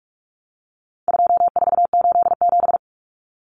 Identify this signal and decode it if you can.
Morse code, heard as a keyed tone, about 44 wpm, 700 Hz. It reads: 2487
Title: Numbers - 4-Digit 45wpm